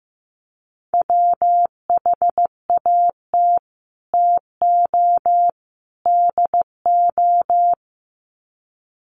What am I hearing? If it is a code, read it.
Morse code, heard as a keyed tone, about 15 wpm, 700 Hz. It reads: WHAT TO DO
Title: Sets of 3 Words 15wpm